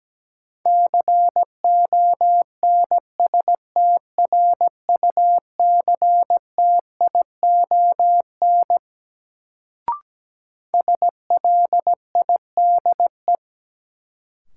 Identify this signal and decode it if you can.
Morse code, heard as a keyed tone, about 17 wpm, 700 Hz. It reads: CONSTRUCTION SLIDE
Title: Sentences from Top 1600 Words - New Word Drill - 17wpm